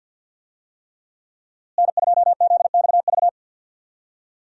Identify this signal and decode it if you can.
Morse code, heard as a keyed tone, about 52 wpm, 700 Hz. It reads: N2ZXV
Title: USA Call Signs - 5 Characters - Repeated 3x - 50wpm